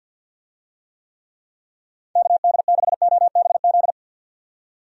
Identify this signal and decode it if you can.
Morse code, heard as a keyed tone, about 46 wpm, 700 Hz. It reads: KD6OBZ
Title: USA Call Signs - 6 Characters - 50wpm